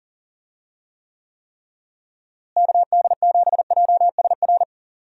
Morse code, heard as a keyed tone, about 37 wpm, 700 Hz. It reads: KD7JSR
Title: USA Call Signs - 6 Characters - Repeated 3x - 40wpm